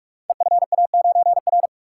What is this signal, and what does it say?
Morse code, heard as a keyed tone, about 44 wpm, 700 Hz. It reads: EFA9R
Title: Code Groups - 5 Letter:Numbers - 45wpm